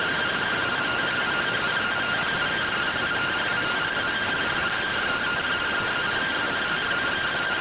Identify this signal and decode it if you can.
Two-tone teleprinter signal, about 170 Hz shift.